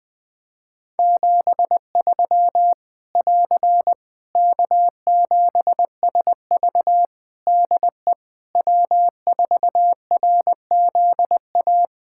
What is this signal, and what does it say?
Morse code, heard as a keyed tone, about 20 wpm, 700 Hz. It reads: 73 <AR> K7SV DE W4RZA
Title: Practice QSO - Exchange 4 - Ending - 20wpm